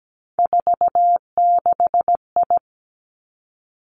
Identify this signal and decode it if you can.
Morse code, heard as a keyed tone, about 17 wpm, 700 Hz. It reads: 46I
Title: Three Challenging Letter-Number Combinations 17wpm